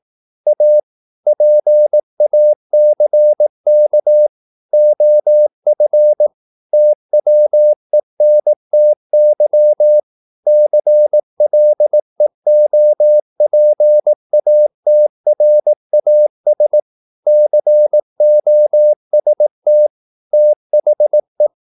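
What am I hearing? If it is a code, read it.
Morse code, heard as a keyed tone, about 18 wpm, 600 Hz. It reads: A PACK OF TWENTY CLEOPATRAS COST THE
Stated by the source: Morse practice at 18 wpm, English text, from CW med Gnister.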